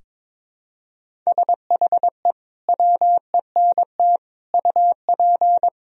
Morse code, heard as a keyed tone, about 22 wpm, 700 Hz. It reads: SHE WENT UP
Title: Sentences from Top 1000 Words - Review Part 1 - 22wpm